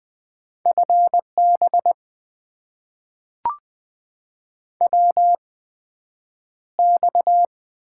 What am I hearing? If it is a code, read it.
Morse code, heard as a keyed tone, about 20 wpm, spaced as for 9 wpm, 700 Hz. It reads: FB WX
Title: Top 90 QSO Elements - 8x Character Spacing - 20wpm